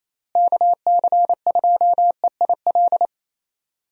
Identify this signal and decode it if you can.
Morse code, heard as a keyed tone, about 28 wpm, 700 Hz. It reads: KC2EIL
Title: USA Call Signs - 6 Characters - Repeated 2x - 28wpm